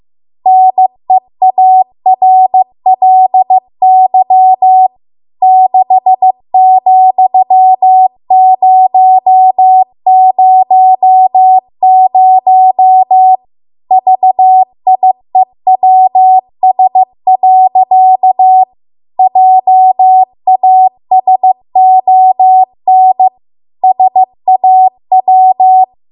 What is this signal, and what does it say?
Morse code, heard as a keyed tone, about 15 wpm, 750 Hz. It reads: NEARLY 6,000 VIEWS. JASON SAW